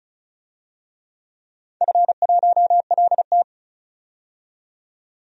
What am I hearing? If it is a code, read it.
Morse code, heard as a keyed tone, about 36 wpm, 690 Hz. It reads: F1LT
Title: DXCC-100 Call Signs - 4 Characters - 35wpm